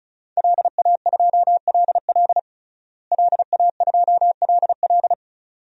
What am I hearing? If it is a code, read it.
Morse code, heard as a keyed tone, about 36 wpm, 700 Hz. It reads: LA2LL LA2LL
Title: DXCC-100 Call Signs - 5 Characters - Repeated 2x - 35wpm